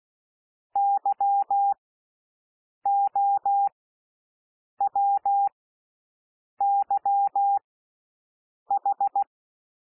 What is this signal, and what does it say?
Morse code, heard as a keyed tone, about 16 wpm, spaced as for 6 wpm, 800 Hz. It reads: YOWYH
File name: SOUND LETTERS 16 wpm